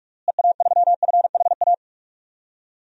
Morse code, heard as a keyed tone, about 46 wpm, 700 Hz. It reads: EA3FHA
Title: DXCC-100 Call Signs - 6 Characters - Repeated 3x - 45wpm